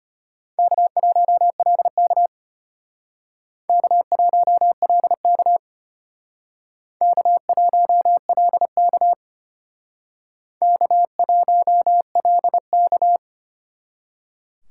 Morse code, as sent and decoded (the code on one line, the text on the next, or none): -.- .---- .-.. -.- / -.- .---- .-.. -.- / -.- .---- .-.. -.- / -.- .---- .-.. -.-
K1LK K1LK K1LK K1LK